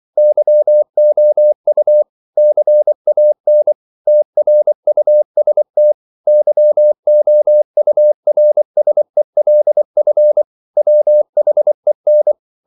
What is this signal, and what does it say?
Morse code, heard as a keyed tone, about 24 wpm, 600 Hz. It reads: YOU CAN TRUST YOURSELF WHEN
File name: CW / Practice / If24